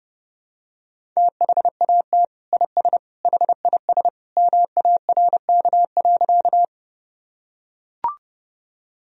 Morse code, heard as a keyed tone, about 30 wpm, 700 Hz. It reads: THAT IS HIS MARK.
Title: Sentences from Top 1000 Words - Review Part 1 - 30wpm